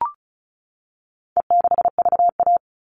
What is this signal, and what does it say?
Morse code, heard as a keyed tone, about 34 wpm, 700 Hz. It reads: E6VA
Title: All Cut Numbers - 4-Digit 35wpm